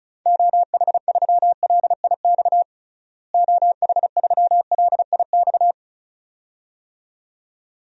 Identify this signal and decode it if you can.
Morse code, heard as a keyed tone, about 36 wpm, 700 Hz. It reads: OH3LIX OH3LIX
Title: DXCC-100 Call Signs - 6 Characters - Repeated 2x - 35wpm